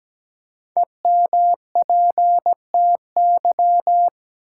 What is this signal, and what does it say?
Morse code, heard as a keyed tone, about 17 wpm, 700 Hz. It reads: EMPTY
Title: Five Letter Words - 17wpm